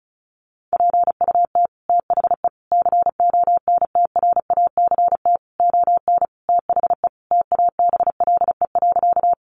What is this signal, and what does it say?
Morse code, heard as a keyed tone, about 34 wpm, 700 Hz. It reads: PUT THE CONTRACT ON THE TABLE.
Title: Sentences from Top 1400 Words 35wpm